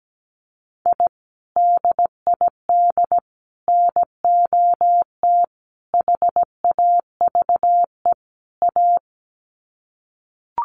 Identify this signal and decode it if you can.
Morse code, heard as a keyed tone, about 17 wpm, 700 Hz. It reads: I DID NOT HAVE A
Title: Sets of 5 Words 17wpm